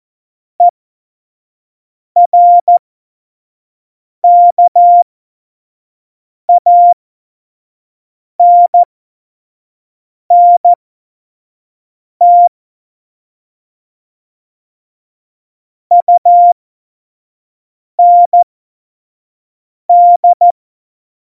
Morse code, heard as a keyed tone, about 14 wpm, spaced as for 5 wpm, 700 Hz. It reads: ERKANNT UND
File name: Morsekurs / Witze / Witz 27 Juni 17, Tempo 25 F70